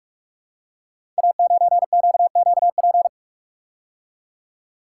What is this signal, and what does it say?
Morse code, heard as a keyed tone, about 44 wpm, 700 Hz. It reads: A9QQP
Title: Code Groups - 5 Letter:Numbers - 45wpm